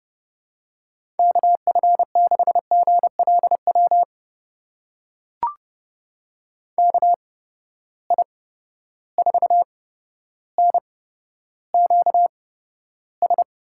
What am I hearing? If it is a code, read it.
Morse code, heard as a keyed tone, about 32 wpm, spaced as for 13 wpm, 700 Hz. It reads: KF6GLW KI4NQS
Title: USA Call Signs - 6 Characters - 8x Character Spacing - 30wpm